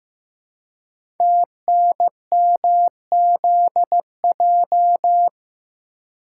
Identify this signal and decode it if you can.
Morse code, heard as a keyed tone, about 15 wpm, 690 Hz. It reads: TNMZJ
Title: Code Groups - 5 Letters - 15wpm